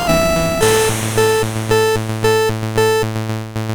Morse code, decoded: none